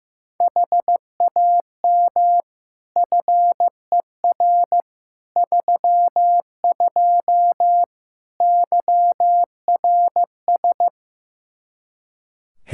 Morse code, heard as a keyed tone, about 15 wpm, 700 Hz. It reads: HAM FER 32 YRS
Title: K1USN SST QSO Practice 15wpm